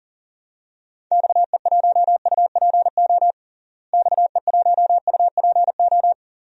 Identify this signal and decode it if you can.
Morse code, heard as a keyed tone, about 40 wpm, 700 Hz. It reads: XE1UPO XE1UPO
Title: DXCC-100 Call Signs - 6 Characters - Repeated 3x - 40wpm